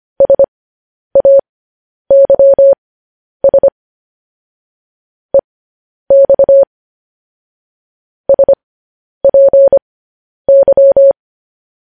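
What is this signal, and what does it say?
Morse code, heard as a keyed tone, about 25 wpm, spaced as for 10 wpm, 550 Hz. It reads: SAYS EX SPY
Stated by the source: Morse code news headlines sent at 10 words per minute.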